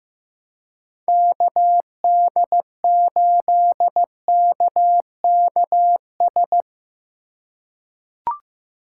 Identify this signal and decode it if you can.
Morse code, heard as a keyed tone, about 15 wpm, 700 Hz. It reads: KD8KKS
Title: USA Call Signs - 2x Character Spacing - 15wpm